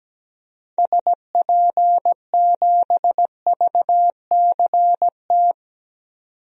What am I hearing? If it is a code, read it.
Morse code, heard as a keyed tone, about 17 wpm, 700 Hz. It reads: SP7VCT
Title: DXCC-100 Call Signs - 6 Characters - 17wpm